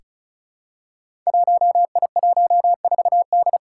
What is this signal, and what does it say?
Morse code, heard as a keyed tone, about 36 wpm, 700 Hz. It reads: 1I14D
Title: Five Challenging Letter-Number Combinations 35wpm